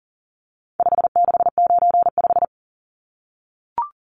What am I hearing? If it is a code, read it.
Morse code, heard as a keyed tone, about 38 wpm, 700 Hz. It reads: 5695
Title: Numbers - 4-Digit 40wpm